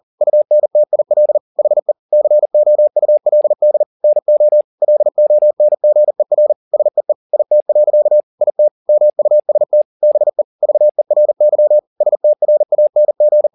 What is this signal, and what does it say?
Morse code, heard as a keyed tone, about 40 wpm, 600 Hz. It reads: UNTIL HE COULD NO LONGER SEE IT. IT MUST BE VERY STRANG
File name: the old man and the sea 2 40wpm